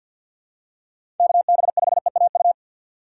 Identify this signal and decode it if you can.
Morse code, heard as a keyed tone, about 50 wpm, 700 Hz. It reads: KB5EAU
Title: USA Call Signs - 6 Characters - 4x Character Spacing - 50wpm